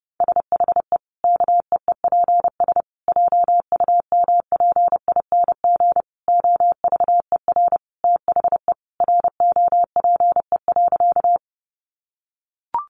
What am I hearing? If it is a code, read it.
Morse code, heard as a keyed tone, about 30 wpm, 700 Hz. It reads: SHE KEEPS JUMPING OVER THE ROPE.
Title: Easy Phrases 30wpm